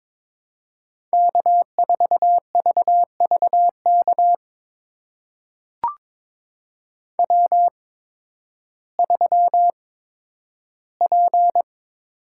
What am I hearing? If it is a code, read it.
Morse code, heard as a keyed tone, about 22 wpm, spaced as for 12 wpm, 700 Hz. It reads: K4VVK W3P
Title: USA Call Signs - 5 Characters - 8x Character Spacing - 22wpm